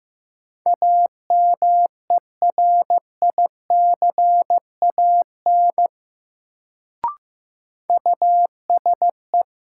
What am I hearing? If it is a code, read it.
Morse code, heard as a keyed tone, about 15 wpm, 700 Hz. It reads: AMERICAN USE
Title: Sets of 2 Words - New Word Drill - 15wpm